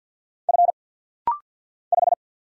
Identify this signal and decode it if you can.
Morse code, heard as a keyed tone, about 50 wpm, 690 Hz. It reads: F 5
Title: Single Letter-Number Alt - Rapid-Fire 50wpm